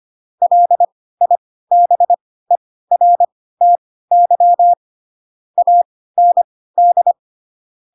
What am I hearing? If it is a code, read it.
Morse code, heard as a keyed tone, about 25 wpm, spaced as for 16 wpm, 700 Hz. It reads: LIBERTY AND
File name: Pinocchio / Pinocchio ch3 10 25 ik1hge